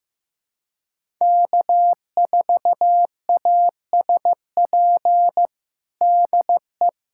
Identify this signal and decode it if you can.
Morse code, heard as a keyed tone, about 15 wpm, 700 Hz. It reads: K4ASP DE
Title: USA Callsign DE Callsign - 5 Characters - 15wpm